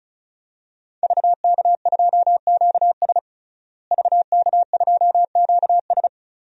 Morse code, heard as a keyed tone, about 36 wpm, 700 Hz. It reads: VK2QS VK2QS
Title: DXCC-100 Call Signs - 5 Characters - Repeated 2x - 35wpm